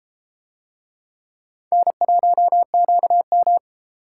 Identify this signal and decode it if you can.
Morse code, heard as a keyed tone, about 33 wpm, 700 Hz. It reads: N1QM